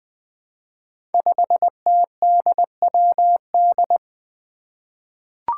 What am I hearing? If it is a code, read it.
Morse code, heard as a keyed tone, about 20 wpm, 700 Hz. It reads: 5TDWD
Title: Five Challenging Letter-Number Combinations 20wpm